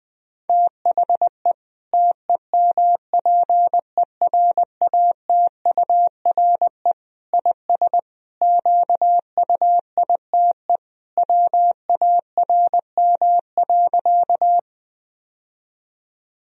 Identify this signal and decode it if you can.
Morse code, heard as a keyed tone, about 20 wpm, 700 Hz. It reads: THE TEMPERATURE IS QUITE WARM.
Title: Sentences from Top 1000 Words - Review Part 4 - 20wpm